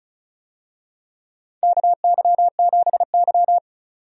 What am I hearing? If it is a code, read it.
Morse code, heard as a keyed tone, about 34 wpm, 700 Hz. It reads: KY7Y
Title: USA Call Signs - 4 Characters - 35wpm